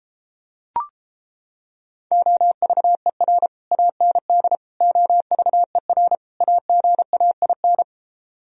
Morse code, heard as a keyed tone, about 31 wpm, 700 Hz. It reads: OVER AND OVER AGAIN